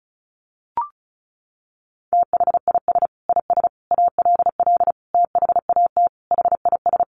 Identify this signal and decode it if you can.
Morse code, heard as a keyed tone, about 34 wpm, 700 Hz. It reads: THIS IS ALL THAT HIS
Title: Sentences from Top 1400 Words 35wpm